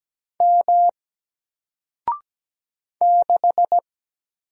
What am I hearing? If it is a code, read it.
Morse code, heard as a keyed tone, about 17 wpm, 700 Hz. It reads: M 6
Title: Single Letter-Number Alt 17wpm